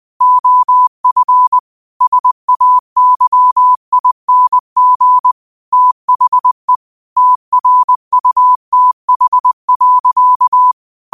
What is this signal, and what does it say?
Morse code, heard as a keyed tone, about 20 wpm, 1000 Hz. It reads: OF SAYING THE TRUTH.